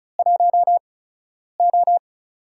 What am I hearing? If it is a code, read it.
Morse code, heard as a keyed tone, about 36 wpm, spaced as for 10 wpm, 700 Hz. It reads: 1O
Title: USA Call Signs - 4 Characters - 8x Character Spacing - 35wpm